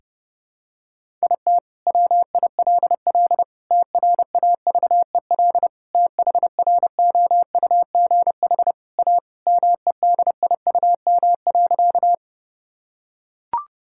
Morse code, heard as a keyed tone, about 28 wpm, 700 Hz. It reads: IT WILL TRAVEL THROUGH A MEDIUM.
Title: Sentences from Top 1000 Words - Review Part 6 - 30wpm